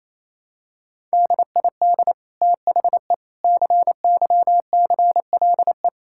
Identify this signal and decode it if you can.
Morse code, heard as a keyed tone, about 28 wpm, 700 Hz. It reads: DID THE CYCLE
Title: Sentences from Top 1000 Words - Review Part 6 - 28wpm